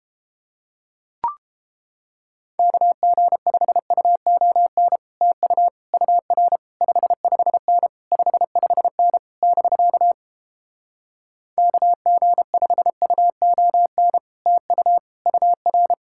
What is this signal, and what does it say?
Morse code, heard as a keyed tone, about 29 wpm, 700 Hz. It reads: KG5UON TU UR 55N 55N <BK> KG5UON TU UR